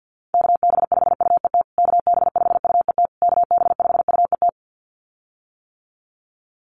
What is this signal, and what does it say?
Morse code, heard as a keyed tone, about 50 wpm, 700 Hz. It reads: KB5UET KB5UET KB5UET
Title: USA Call Signs Repeated 3x 50wpm